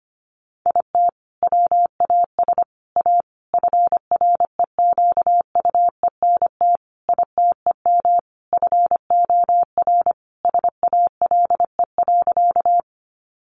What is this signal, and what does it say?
Morse code, heard as a keyed tone, about 25 wpm, 700 Hz. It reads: IT WAS A FREQUENT ITEM FOR SALE.
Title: Sentences from Top 1400 Words 25wpm